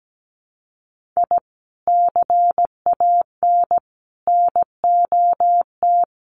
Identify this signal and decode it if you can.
Morse code, heard as a keyed tone, about 17 wpm, 700 Hz. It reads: I CAN NOT
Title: Sets of 5 Words Encore2 17wpm